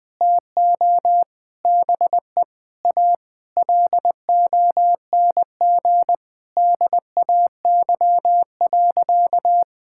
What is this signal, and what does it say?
Morse code, heard as a keyed tone, about 20 wpm, 700 Hz. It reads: TO BE A LONG DAY.
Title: Sentences with 5-Word Phrases - 20wpm